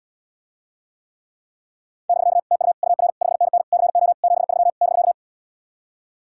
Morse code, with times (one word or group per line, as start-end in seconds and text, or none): 2.1-2.4 <BK>
2.5-2.7 TU
2.8-3.1 UR
3.2-3.6 5NN
3.7-4.1 CO
4.2-4.7 73
4.8-5.1 <BK>